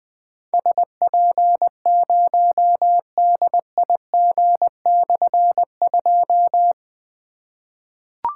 Morse code, heard as a keyed tone, about 20 wpm, 700 Hz. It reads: SP0DIG/2
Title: Difficult Call Signs 20wpm